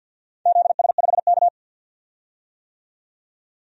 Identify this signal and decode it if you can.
Morse code, heard as a keyed tone, about 49 wpm, 700 Hz. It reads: ZS5K
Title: DXCC-100 Call Signs - 4 Characters - 50wpm